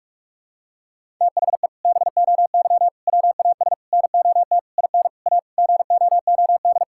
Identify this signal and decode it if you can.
Morse code, heard as a keyed tone, about 44 wpm, 700 Hz. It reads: THE BOY WAS NOT IN A GOOD
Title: Sentences from Top 1000 Words - Review Part 6 - 45wpm